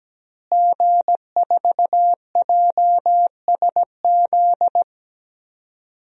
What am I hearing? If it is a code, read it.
Morse code, heard as a keyed tone, about 17 wpm, 700 Hz. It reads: G4JSZ
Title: DXCC-100 Call Signs - 5 Characters - Repeated 3x - 17wpm